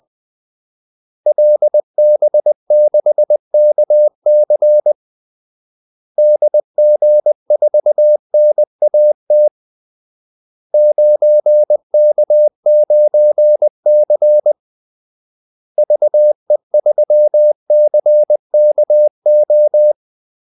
Morse code, heard as a keyed tone, about 20 wpm, 600 Hz. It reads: LB6KC DG4NAT 9K9C VE3CKO
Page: Kaldesignaler 20 wpm